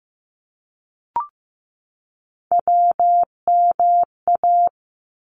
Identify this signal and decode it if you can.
Morse code, heard as a keyed tone, about 15 wpm, 700 Hz. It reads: WMA
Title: ARRL Sections 15wpm